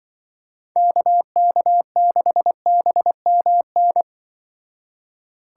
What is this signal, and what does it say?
Morse code, heard as a keyed tone, about 24 wpm, 700 Hz. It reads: KK6BMN
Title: Speed Racing - USA 2x3 Call Signs 20wpm